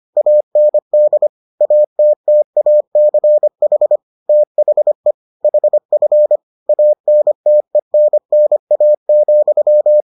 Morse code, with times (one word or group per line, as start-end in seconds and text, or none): 0.2-1.3 AND
1.6-4.0 ATTACH
4.3-5.1 THE
5.4-6.4 HF
6.7-10.0 ANTENNA,